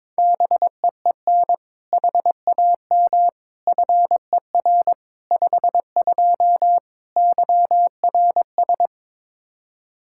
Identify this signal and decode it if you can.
Morse code, heard as a keyed tone, about 21 wpm, 700 Hz. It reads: BEEN HAM FER 52 YRS
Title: Practice QSO - Exchange 3 - More Info - 22wpm